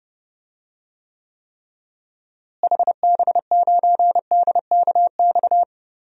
Morse code, heard as a keyed tone, about 31 wpm, 700 Hz. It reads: HB9DKX
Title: DXCC-100 Call Signs - 6 Characters - Repeated 3x - 30wpm